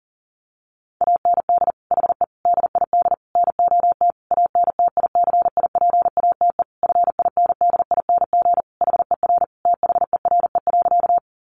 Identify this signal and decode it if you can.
Morse code, heard as a keyed tone, about 40 wpm, 700 Hz. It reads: AND HE DID NOT ANTICIPATE FINDING HER THERE.
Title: Sentences from Top 1500 Words 40wpm